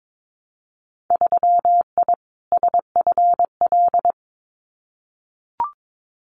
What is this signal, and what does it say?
Morse code, heard as a keyed tone, about 22 wpm, 700 Hz. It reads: 3I SFL
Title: Winter Field Day Exchanges 22wpm